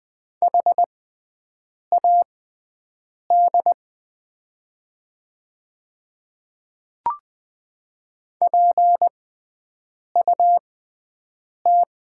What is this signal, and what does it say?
Morse code, heard as a keyed tone, about 20 wpm, spaced as for 7 wpm, 700 Hz. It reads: HAD PUT
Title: 6x Character Spacing - Top 300 Words - 20wpm